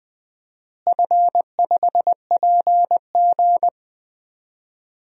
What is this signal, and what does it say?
Morse code, heard as a keyed tone, about 20 wpm, 700 Hz. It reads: F5PG
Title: DXCC-100 Call Signs - 4 Characters - Repeated 3x - 20wpm